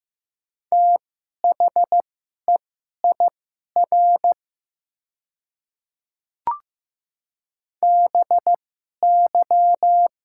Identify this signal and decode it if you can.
Morse code, heard as a keyed tone, about 15 wpm, spaced as for 11 wpm, 700 Hz. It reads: THEIR BY
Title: 2x Character Spacing - Top 100 Words - 15wpm